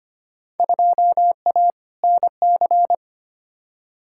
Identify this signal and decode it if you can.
Morse code, heard as a keyed tone, about 25 wpm, 700 Hz. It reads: 2A NC
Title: Field Day Exchanges - 25wpm